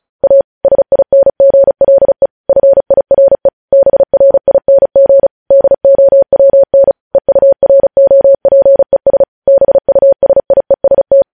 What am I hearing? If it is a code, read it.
Morse code, heard as a keyed tone, about 36 wpm, 550 Hz. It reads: A SINGLE FIRE BRING DOWN EUROPES BUSIEST
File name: podcast / News headlines-1x-35x35